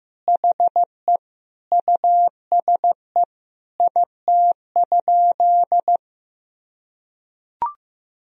Morse code, heard as a keyed tone, about 15 wpm, 700 Hz. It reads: HE USE IT?
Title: Sentences from Top 200 Words and Abbreviations 15wpm